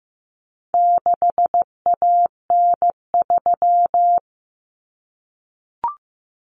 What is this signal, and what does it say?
Morse code, heard as a keyed tone, about 15 wpm, 700 Hz. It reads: 6AN3
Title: Common Cut Numbers - 4-Digit 15wpm